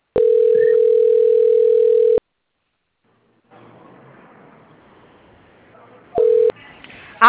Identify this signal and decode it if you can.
Ringback tone (North America), 440 + 480 Hz.